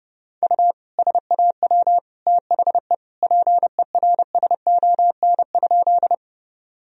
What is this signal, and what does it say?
Morse code, heard as a keyed tone, about 30 wpm, 700 Hz. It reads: U SAW THE PERSON?
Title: Sentences from Top 400 Words and Abbreviations 30wpm